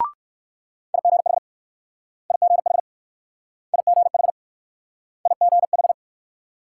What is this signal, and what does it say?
Morse code, heard as a keyed tone, about 56 wpm, 700 Hz. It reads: IGH IGH IGH IGH
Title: Speed-Racing Multi-letter Phonemes - 45wpm